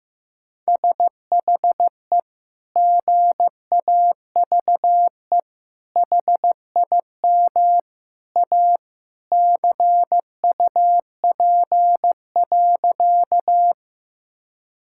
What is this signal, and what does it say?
Morse code, heard as a keyed tone, about 15 wpm, 700 Hz. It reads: SHE GAVE HIM A CUP.
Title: Sentences from Top 1600 Words 15wpm